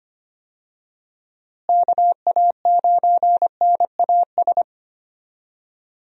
Morse code, heard as a keyed tone, about 25 wpm, 700 Hz. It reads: KA9NAS
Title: USA Call Signs - 6 Characters - Repeated 2x - 25wpm